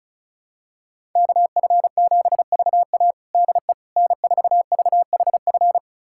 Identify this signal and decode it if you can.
Morse code, heard as a keyed tone, about 34 wpm, 700 Hz. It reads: KF7VA DE N4VHF
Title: USA Callsign DE Callsign - 5 Characters - 35wpm